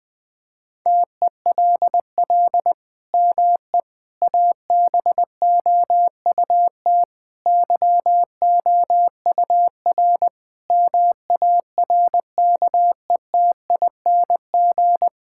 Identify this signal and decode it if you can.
Morse code, heard as a keyed tone, about 20 wpm, 700 Hz. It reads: TELL ME ABOUT YOUR MARKETING
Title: Sentences from Top 1000 Words - Review Part 5 - 20wpm